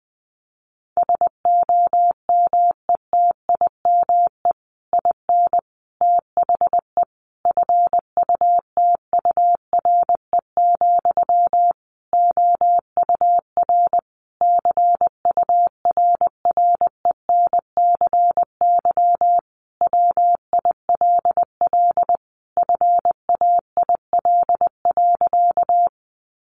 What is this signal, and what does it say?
Morse code, heard as a keyed tone, about 20 wpm, 700 Hz. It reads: SOMETIME IN THE FUTURE, OUR CURRENCY WILL FAIL.
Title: Sentences from Top 1500 Words 20wpm